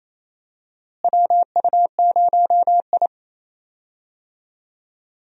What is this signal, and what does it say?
Morse code, heard as a keyed tone, about 28 wpm, 700 Hz. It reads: WU0I
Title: USA Call Signs - 4 Characters - Repeated 3x - 28wpm